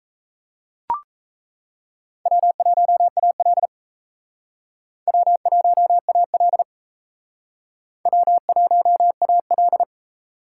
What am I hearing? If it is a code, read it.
Morse code, heard as a keyed tone, about 38 wpm, 700 Hz. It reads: W1AL W1AL W1AL